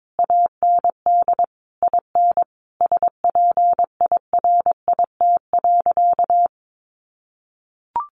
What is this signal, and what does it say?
Morse code, heard as a keyed tone, about 22 wpm, 700 Hz. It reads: AND IN SPIRIT.
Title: Sentences from Top 1400 Words 22wpm